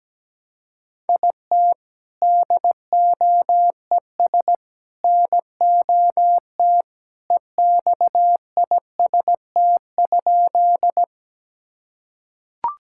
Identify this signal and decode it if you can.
Morse code, heard as a keyed tone, about 17 wpm, 700 Hz. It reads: IT DOES NOT EXIST?
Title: Sentences from Top 1000 Words - Review Part 5 - 17wpm